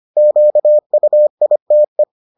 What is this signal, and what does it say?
Morse code, heard as a keyed tone, about 25 wpm, 600 Hz. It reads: QUITE